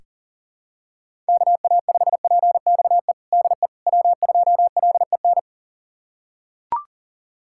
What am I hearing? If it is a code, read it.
Morse code, heard as a keyed tone, about 40 wpm, 700 Hz. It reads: KA5PXE DE W2LEN
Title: USA Callsign DE Callsign 40wpm